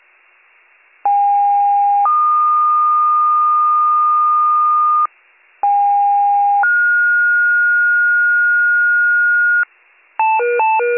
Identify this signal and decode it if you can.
Two-tone page, heard as a one-second tone, then a three-second tone, about 810 then 1220 Hz, about 810 then 1430 Hz, with noise. It is heard from 1.0 to 9.6 s.